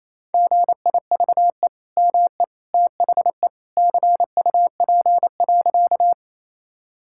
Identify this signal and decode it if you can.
Morse code, heard as a keyed tone, about 28 wpm, 700 Hz. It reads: GIVE ME THE CUP.
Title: Sentences from Top 1600 Words 28wpm